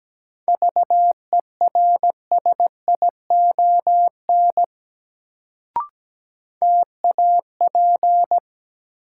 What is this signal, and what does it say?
Morse code, heard as a keyed tone, about 17 wpm, 700 Hz. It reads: VERSION TAP
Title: Sentences from Top 1400 Words - New Word Drill - 17wpm